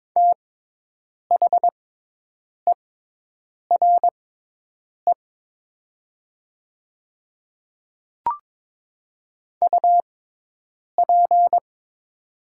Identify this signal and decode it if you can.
Morse code, heard as a keyed tone, about 22 wpm, spaced as for 8 wpm, 700 Hz. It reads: THERE UP
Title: 6x Character Spacing - Top 100 Words - 22wpm